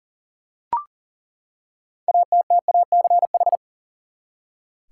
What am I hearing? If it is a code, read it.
Morse code, heard as a keyed tone, about 40 wpm, 700 Hz. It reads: ATTACH
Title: Six Letter Words - 40wpm